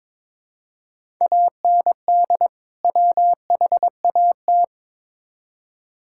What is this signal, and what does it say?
Morse code, heard as a keyed tone, about 22 wpm, 700 Hz. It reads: AND WHAT
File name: Build A Phrase - 2-Word Phrases - 22wpm